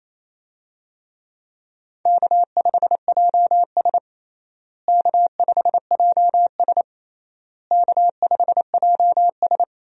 Morse code, heard as a keyed tone, about 28 wpm, 700 Hz. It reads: K5JS K5JS K5JS
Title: USA Call Signs - 4 Characters - Repeated 3x - 28wpm